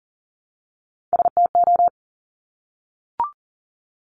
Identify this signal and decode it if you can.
Morse code, heard as a keyed tone, about 38 wpm, 690 Hz. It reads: STO
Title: Top Three Letter Combinations in Words 40wpm